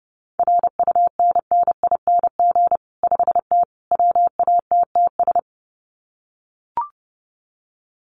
Morse code, heard as a keyed tone, about 30 wpm, 700 Hz. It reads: RUNNING 5T WATTS
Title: Common QSO Phrases 30wpm